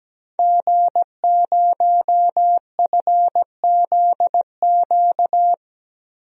Morse code, heard as a keyed tone, about 17 wpm, 700 Hz. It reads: G0FZQ
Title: DXCC-100 Call Signs - 5 Characters - 17wpm